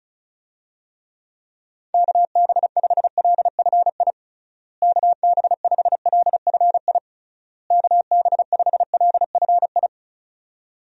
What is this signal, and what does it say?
Morse code, heard as a keyed tone, about 36 wpm, 700 Hz. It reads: KB5LFI KB5LFI KB5LFI
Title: USA Call Signs - 6 Characters - Repeated 3x - 35wpm